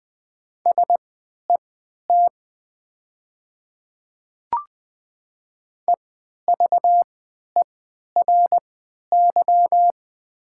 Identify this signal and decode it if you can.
Morse code, heard as a keyed tone, about 20 wpm, spaced as for 11 wpm, 700 Hz. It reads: SET EVERY
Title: 3x Character Spacing - Top 200 Words - 20wpm